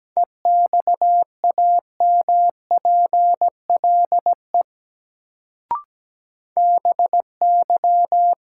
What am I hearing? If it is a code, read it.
Morse code, heard as a keyed tone, about 17 wpm, 700 Hz. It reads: EXAMPLE BY
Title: Sets of 2 Words - New Word Drill - 17wpm